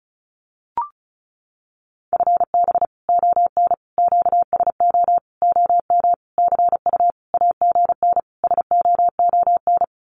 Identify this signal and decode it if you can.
Morse code, heard as a keyed tone, about 34 wpm, 700 Hz. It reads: FB ON QSO OM CU AGN SOON
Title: Common QSO Phrases 35wpm